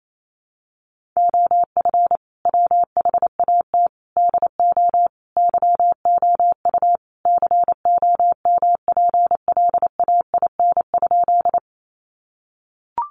Morse code, heard as a keyed tone, about 28 wpm, 700 Hz. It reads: OF WHAT DO YOU COMPLAIN?
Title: Sentences from Top 1300 Words 28wpm